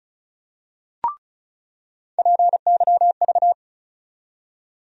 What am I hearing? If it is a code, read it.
Morse code, heard as a keyed tone, about 36 wpm, 700 Hz. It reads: PYV
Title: USA Call Sign Postfixes - 3 Letters 35wpm